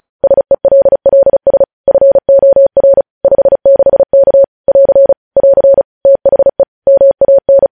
Morse code, heard as a keyed tone, about 36 wpm, 550 Hz. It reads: SELLS FOR 56K <AR> <AR> THE MAN
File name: podcast / News headlines-1x-35x35